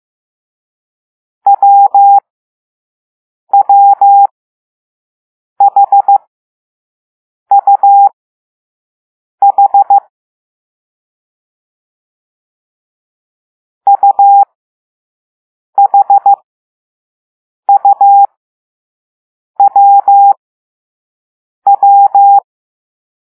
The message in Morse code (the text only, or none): WWHUH UHUWW